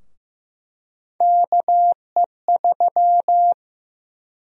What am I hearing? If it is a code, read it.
Morse code, heard as a keyed tone, about 15 wpm, 700 Hz. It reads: KE3
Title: USA Call Sign Prefixes - 15wpm